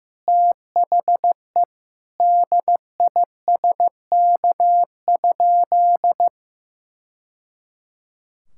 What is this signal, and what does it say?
Morse code, heard as a keyed tone, about 15 wpm, 700 Hz. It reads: THE DISK?
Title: Sentences from Top 1000 Words - Review Part 6 - 15wpm